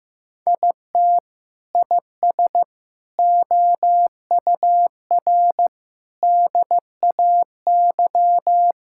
Morse code, heard as a keyed tone, about 15 wpm, 700 Hz. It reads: IT IS OUR DAY
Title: Sentences from Top 1000 Words - Review Part 5 - 15wpm